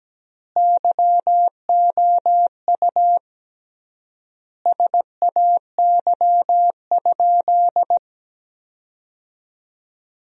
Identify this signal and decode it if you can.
Morse code, heard as a keyed tone, about 17 wpm, 700 Hz. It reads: YOU SAY?
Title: 3x Word-Spacing - Sentences from Top 200 Words - 17wpm